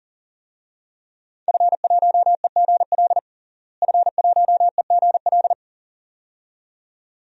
Morse code, heard as a keyed tone, about 40 wpm, 700 Hz. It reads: F1EGL F1EGL
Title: DXCC-100 Call Signs - 5 Characters - Repeated 2x - 40wpm